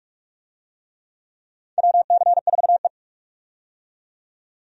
Morse code, heard as a keyed tone, about 46 wpm, 700 Hz. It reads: WC4E
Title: USA Call Signs - 4 Characters - 45wpm